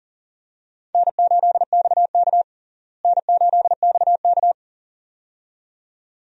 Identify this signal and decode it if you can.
Morse code, heard as a keyed tone, about 40 wpm, 700 Hz. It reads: N8XK N8XK
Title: USA Call Signs - 4 Characters - Repeated 2x - 40wpm